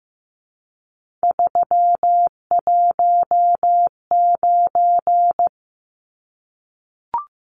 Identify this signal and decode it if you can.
Morse code, heard as a keyed tone, about 15 wpm, 700 Hz. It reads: 319
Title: Numbers - 3-Digit 15wpm